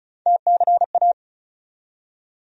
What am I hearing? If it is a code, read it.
Morse code, heard as a keyed tone, about 34 wpm, 700 Hz. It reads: TCA
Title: USA Call Sign Postfixes - 3 Letters 35wpm